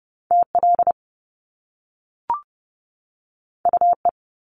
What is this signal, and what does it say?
Morse code, heard as a keyed tone, about 30 wpm, 700 Hz. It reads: TL UE
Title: Top Two Letter Combinations in Words 30wpm